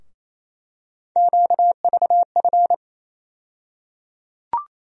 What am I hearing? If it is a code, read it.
Morse code, heard as a keyed tone, about 28 wpm, 700 Hz. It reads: QVF
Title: USA Call Sign Postfixes - 3 Letters 28wpm